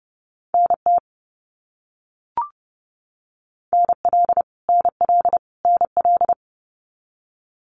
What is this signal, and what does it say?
Morse code, heard as a keyed tone, about 30 wpm, 700 Hz. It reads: NT NL NL NL
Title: Canadian Provinces and Territories - Repeated 3x - 30wpm